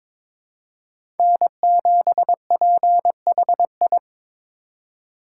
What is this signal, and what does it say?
Morse code, heard as a keyed tone, about 22 wpm, 700 Hz. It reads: N7PHI
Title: K1USN SST QSO Practice - 22wpm